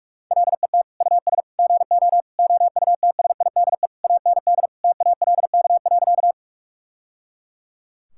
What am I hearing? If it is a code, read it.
Morse code, heard as a keyed tone, about 42 wpm, 700 Hz. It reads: LET US GO OUTSIDE AND TALK.
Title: Sentences from Top 1000 Words - Review Part 6 - 45wpm